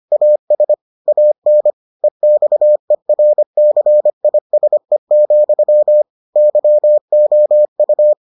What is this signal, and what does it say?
Morse code, heard as a keyed tone, about 25 wpm, 600 Hz. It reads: AS AN EXERCISE, YOU